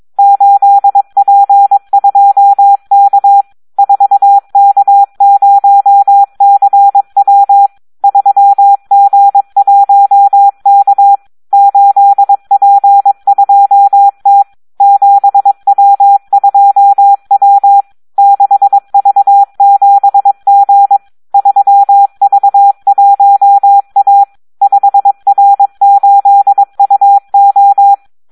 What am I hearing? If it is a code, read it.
Morse code, heard as a keyed tone, about 22 wpm, 800 Hz. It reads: 8P2K 4K0CW 3G1K 8P2T 7W2W 6V7G 3V1A 5R8UO
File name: Audio practica CW / SD 1 22